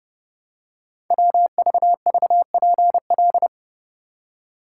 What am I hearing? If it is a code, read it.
Morse code, heard as a keyed tone, about 30 wpm, 700 Hz. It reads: WVVPL
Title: Code Groups - 5 Letter:Numbers - 30wpm